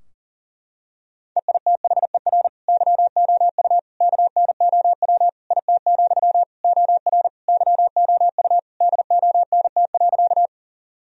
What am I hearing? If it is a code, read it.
Morse code, heard as a keyed tone, about 40 wpm, 700 Hz. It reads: EITHER YOU KNOW IT, OR YOU DONT.
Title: Sentences from Top 1000 Words - Review Part 4 - 40wpm